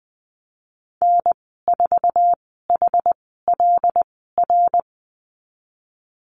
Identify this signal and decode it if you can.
Morse code, heard as a keyed tone, about 20 wpm, spaced as for 15 wpm, 700 Hz. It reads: N4HLR
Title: 167 Lesson 27g - Number 7 - Call Signs 20wpm